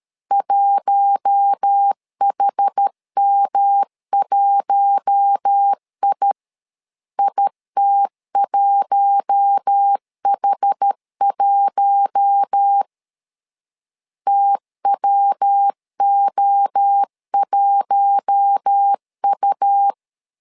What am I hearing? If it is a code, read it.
Morse code, heard as a keyed tone, about 13 wpm, 800 Hz. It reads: 1HM1I IT1H1 TWO1U
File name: LEZIONI / 4 Q1 12wpm 2028